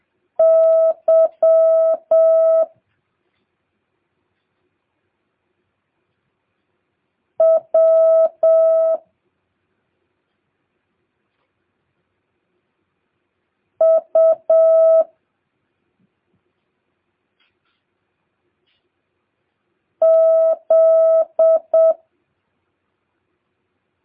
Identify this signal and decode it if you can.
Morse code, heard as a keyed tone, about 7 wpm, 650 Hz. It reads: Y W U Z